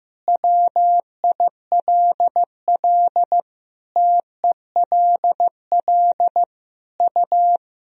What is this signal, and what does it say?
Morse code, heard as a keyed tone, about 15 wpm, 700 Hz. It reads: WILL TELL U
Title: Sentences from Top 300 Words and Abbreviations 15wpm